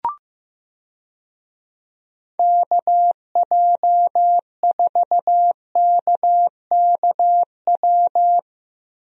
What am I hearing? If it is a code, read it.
Morse code, heard as a keyed tone, about 15 wpm, 700 Hz. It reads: KJ4KKW
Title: USA Call Signs - 6 Characters - 15wpm